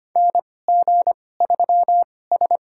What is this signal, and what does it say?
Morse code, heard as a keyed tone, about 25 wpm, spaced as for 18 wpm, 700 Hz. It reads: NG3S
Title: USA Call Signs - 4 Characters - 2x Character Spacing - 25wpm